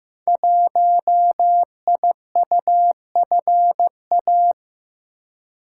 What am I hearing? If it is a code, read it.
Morse code, heard as a keyed tone, about 15 wpm, 700 Hz. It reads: 1IUFA
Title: Code Groups - 5 Letter:Numbers - 15wpm